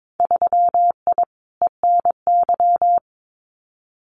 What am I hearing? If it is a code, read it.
Morse code, heard as a keyed tone, about 22 wpm, 700 Hz. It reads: 3I ENY
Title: Winter Field Day Exchanges Encore 22wpm